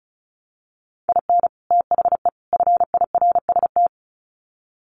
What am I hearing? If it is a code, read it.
Morse code, heard as a keyed tone, about 34 wpm, 700 Hz. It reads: IN THE FIRST
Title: Sets of 3 Words 35wpm